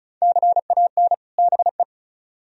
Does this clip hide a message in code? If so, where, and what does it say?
Yes: Morse code from 0.2 to 1.8 s, heard as a keyed tone, about 46 wpm, 700 Hz. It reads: CAN BE